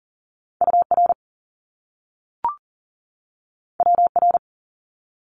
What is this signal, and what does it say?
Morse code, heard as a keyed tone, about 40 wpm, 700 Hz. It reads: UR WR
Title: Top Two Letter Combinations in Words 40wpm